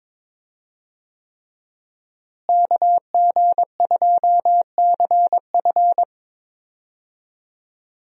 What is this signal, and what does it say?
Morse code, heard as a keyed tone, about 22 wpm, 700 Hz. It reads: KG2CF
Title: USA Call Signs - 5 Characters - Repeated 3x - 22wpm